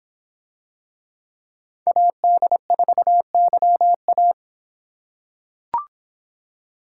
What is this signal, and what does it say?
Morse code, heard as a keyed tone, about 24 wpm, 700 Hz. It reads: AD4YA